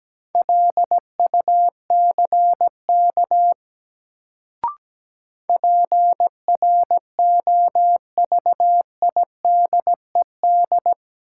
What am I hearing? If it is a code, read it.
Morse code, heard as a keyed tone, about 17 wpm, 700 Hz. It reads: LUCK PROVIDED
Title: Sentences from Top 1500 Words - New Word Drill Alt - 17wpm